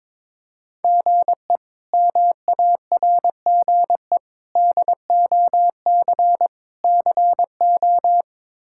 Morse code, heard as a keyed tone, about 22 wpm, 700 Hz. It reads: GE MARGE DOC CO
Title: K1USN SST QSO Practice - 22wpm